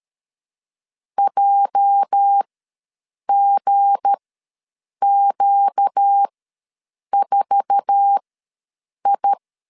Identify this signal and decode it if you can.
Morse code, heard as a keyed tone, about 12 wpm, spaced as for 7 wpm, 800 Hz. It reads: JGQ4I
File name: LEZIONI / 9 2867 ws 12wpm 3431